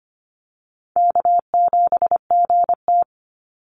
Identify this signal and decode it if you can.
Morse code, heard as a keyed tone, about 25 wpm, 700 Hz. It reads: K7GT
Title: NA SOTA Chaser Callsigns 25wpm